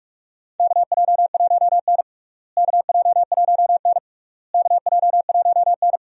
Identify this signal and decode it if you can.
Morse code, heard as a keyed tone, about 46 wpm, 700 Hz. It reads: KJ1N KJ1N KJ1N
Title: USA Call Signs - 4 Characters - Repeated 3x - 45wpm